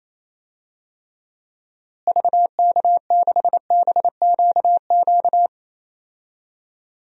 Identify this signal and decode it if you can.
Morse code, heard as a keyed tone, about 28 wpm, 700 Hz. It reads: VK6BQQ
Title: DXCC-100 Call Signs - 6 Characters - Repeated 2x - 28wpm